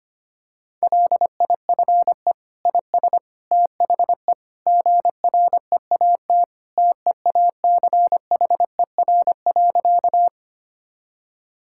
Morse code, heard as a keyed tone, about 25 wpm, 700 Hz. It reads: LIFE IS THE GREAT TEACHER.
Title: Sentences from Top 1000 Words - Review Part 6 - 25wpm